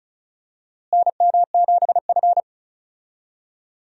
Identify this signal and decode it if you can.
Morse code, heard as a keyed tone, about 34 wpm, 700 Hz. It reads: NM7F
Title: USA Call Signs - 4 Characters - Repeated 3x - 35wpm